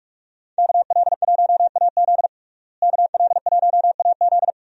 Morse code, heard as a keyed tone, about 45 wpm, 700 Hz. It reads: KL1AZ KL1AZ
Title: USA Call Signs - 5 Characters - Repeated 3x - 45wpm